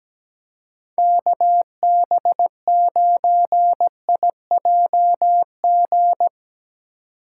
Morse code, heard as a keyed tone, about 18 wpm, 700 Hz. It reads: KB9IJG
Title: Speed Racing - USA 2x3 Call Signs 17wpm